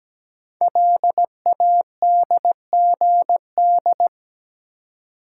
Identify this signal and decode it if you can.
Morse code, heard as a keyed tone, about 17 wpm, 700 Hz. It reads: LADGD
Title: Code Groups - 5 Letters - 17wpm